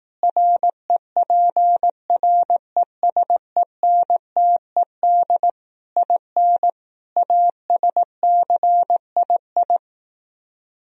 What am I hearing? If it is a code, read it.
Morse code, heard as a keyed tone, about 18 wpm, 700 Hz. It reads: REPRESENTED IN ASCII
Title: Speed-Racing - Sets of 5 Words 15wpm